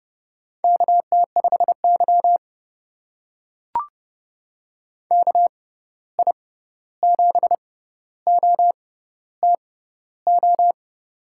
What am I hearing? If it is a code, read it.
Morse code, heard as a keyed tone, about 30 wpm, spaced as for 14 wpm, 700 Hz. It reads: KT5Y KI7OTO
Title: USA Call Signs - 6x Character Spacing - 30wpm